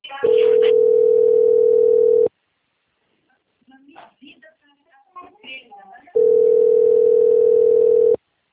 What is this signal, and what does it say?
Ringback tone (North America), 440 + 480 Hz, with speech.
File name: out- - - - - .15328